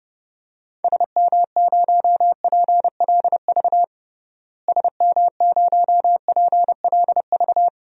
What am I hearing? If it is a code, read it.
Morse code, heard as a keyed tone, about 32 wpm, 700 Hz. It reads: SM0PLV SM0PLV
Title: DXCC-100 Call Signs - 6 Characters - Repeated 2x - 30wpm